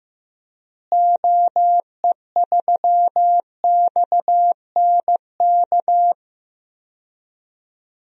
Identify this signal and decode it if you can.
Morse code, heard as a keyed tone, about 15 wpm, 700 Hz. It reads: OE3XNK
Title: DXCC-100 Call Signs - 6 Characters - Repeated 3x - 15wpm